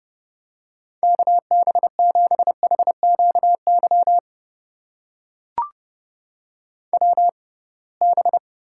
Morse code, heard as a keyed tone, about 31 wpm, spaced as for 23 wpm, 700 Hz. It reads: KB7HQY WB
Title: USA Call Signs - 6 Characters - 6x Character Spacing - 30wpm